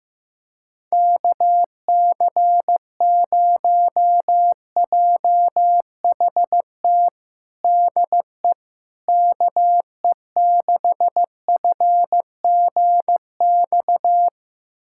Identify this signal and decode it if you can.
Morse code, heard as a keyed tone, about 15 wpm, 700 Hz. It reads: KC0JHT DE KE6FGX
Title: USA Callsign DE Callsign - 6 Characters - 15wpm